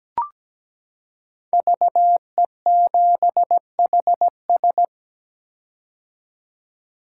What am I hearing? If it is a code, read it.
Morse code, heard as a keyed tone, about 17 wpm, 700 Hz. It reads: VE7HS
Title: K1USN SST QSO Practice - 17wpm